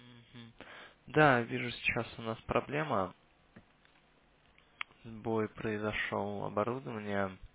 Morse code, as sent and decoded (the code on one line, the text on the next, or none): none